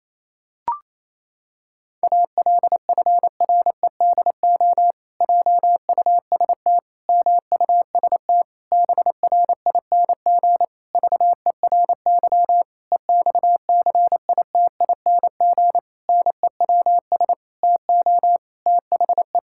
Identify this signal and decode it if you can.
Morse code, heard as a keyed tone, about 28 wpm, 700 Hz. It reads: ALFREDO JUST MUST BRING VERY EXCITING NEWS TO THE
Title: Pangram Challenge v2 - 28wpm